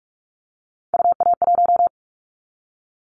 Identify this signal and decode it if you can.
Morse code, heard as a keyed tone, about 46 wpm, 700 Hz. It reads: UA1
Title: Callsign Prefixes for Top 50 DXCC - 45wpm